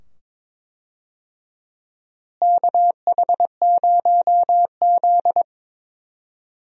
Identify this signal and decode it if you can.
Morse code, heard as a keyed tone, about 22 wpm, 700 Hz. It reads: KH0Z
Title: USA Call Signs - 4 Characters - Repeated 2x - 22wpm